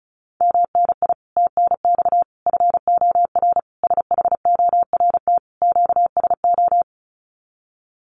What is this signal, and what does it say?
Morse code, heard as a keyed tone, about 34 wpm, 700 Hz. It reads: MNI TNX FOR SHORT QSO
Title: Common QSO Phrases 35wpm